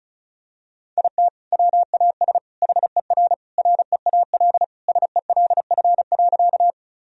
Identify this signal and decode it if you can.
Morse code, heard as a keyed tone, about 36 wpm, 700 Hz. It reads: IT WAS HER REAL SELF.
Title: Sentences from Top 1000 Words - Review Part 1 - 35wpm